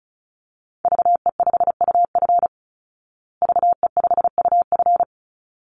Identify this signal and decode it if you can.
Morse code, heard as a keyed tone, about 34 wpm, 700 Hz. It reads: VE5UF VE5UF
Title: Call Signs Repeated 3x - 35wpm